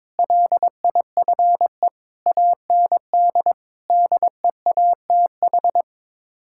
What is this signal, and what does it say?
Morse code, heard as a keyed tone, about 22 wpm, 700 Hz. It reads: LIFE AND DEATH
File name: practice / 22wpm / Sets of 8 Words - 22wpm